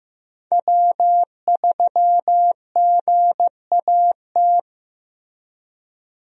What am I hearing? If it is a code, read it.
Morse code, heard as a keyed tone, about 15 wpm, 700 Hz. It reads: W3GAT
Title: USA Call Signs - 5 Characters - Repeated 3x - 15wpm